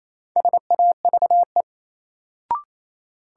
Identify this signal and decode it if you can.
Morse code, heard as a keyed tone, about 28 wpm, 700 Hz. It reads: SAVE
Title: Sentences from Top 1100 Words - New Word Drill Alt - 28wpm